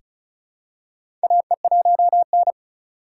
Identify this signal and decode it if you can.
Morse code, heard as a keyed tone, about 36 wpm, 700 Hz. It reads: AE1N
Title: K1USN SST QSO Practice - 35wpm